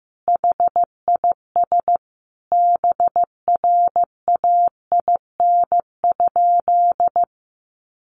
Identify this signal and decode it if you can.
Morse code, heard as a keyed tone, about 15 wpm, 700 Hz. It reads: HIS BRAIN?
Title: Sentences from Top 1100 Words 15wpm